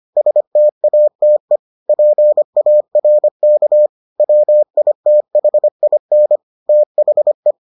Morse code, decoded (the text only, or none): STATE PARK WITHIN THE